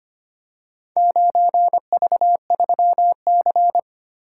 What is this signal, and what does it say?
Morse code, heard as a keyed tone, about 25 wpm, 700 Hz. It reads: 9V3C
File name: practice / 25wpm / Four Letter-Number Combinations - 25wpm